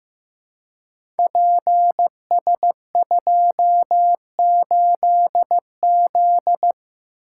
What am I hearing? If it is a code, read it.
Morse code, heard as a keyed tone, about 15 wpm, 700 Hz. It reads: PS28Z
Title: Code Groups - 5 Letter:Numbers - 15wpm